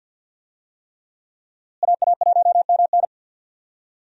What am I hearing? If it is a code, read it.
Morse code, heard as a keyed tone, about 52 wpm, 700 Hz. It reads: AA1MN
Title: USA Call Signs - 5 Characters - Repeated 3x - 50wpm